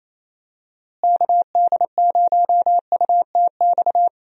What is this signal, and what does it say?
Morse code, heard as a keyed tone, about 28 wpm, 700 Hz. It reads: KD0UTX
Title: USA Call Signs - 6 Characters - Repeated 2x - 28wpm